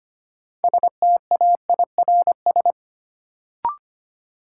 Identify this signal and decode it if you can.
Morse code, heard as a keyed tone, about 25 wpm, 700 Hz. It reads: STAIRS
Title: Sets of 5 Words Encore - New Word Drill - 25wpm